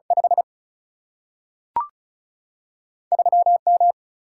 Morse code, heard as a keyed tone, about 36 wpm, 700 Hz. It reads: 5 3 M
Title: Single-Letter Single-Number 35wpm